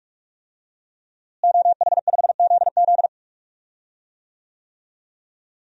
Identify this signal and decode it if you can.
Morse code, heard as a keyed tone, about 43 wpm, 700 Hz. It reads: OH5ZZ
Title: DXCC-100 Call Signs - 5 Characters - Repeated 3x - 45wpm